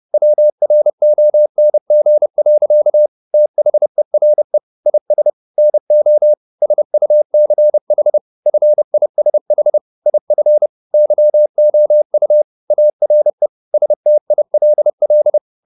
Morse code, heard as a keyed tone, about 30 wpm, 600 Hz. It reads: WRONG. THERE IS NO SUCH FISH IF YOU ARE STILL
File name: the old man and the sea / the old man and the sea 1 30wpm